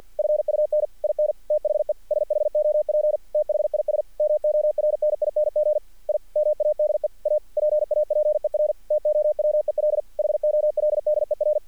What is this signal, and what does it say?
Morse code, heard as a keyed tone, about 50 wpm, 600 Hz. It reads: FUN IN THE SHOP THIS MORNING I MADE A PAPER TOWEL HOLDER
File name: archived / CRSnet